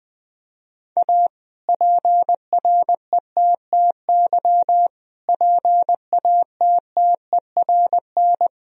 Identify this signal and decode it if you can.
Morse code, heard as a keyed tone, about 21 wpm, 700 Hz. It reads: A PRETTY PATTERN
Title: Sentences from Top 1000 Words - Review Part 2 - 20wpm